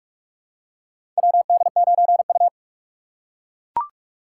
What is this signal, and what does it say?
Morse code, heard as a keyed tone, about 44 wpm, 690 Hz. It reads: WD9U
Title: USA Call Signs - 4 Characters - 4x Character Spacing - 45wpm